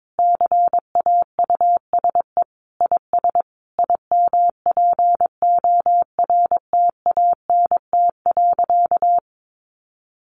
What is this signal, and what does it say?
Morse code, heard as a keyed tone, about 22 wpm, 700 Hz. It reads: CAUSE IS IMPORTANT.
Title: Sentences from Top 1200 Words 22wpm